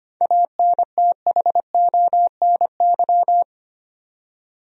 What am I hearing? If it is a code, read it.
Morse code, heard as a keyed tone, about 25 wpm, 700 Hz. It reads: ANTHONY
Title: USA Common Names - 25wpm